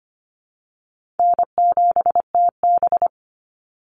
Morse code, heard as a keyed tone, about 25 wpm, 700 Hz. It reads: N7TB
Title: Easy Call Signs Repeated 2x - 25wpm